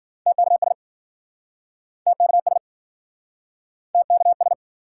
Morse code, heard as a keyed tone, about 52 wpm, 700 Hz. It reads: TKS TKS TKS